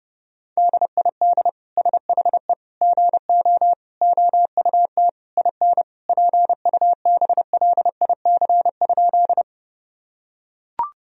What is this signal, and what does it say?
Morse code, heard as a keyed tone, about 30 wpm, 700 Hz. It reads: DID SHE GO OUT IN PUBLIC?
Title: Sentences from Top 1000 Words - Review Part 3 - 30wpm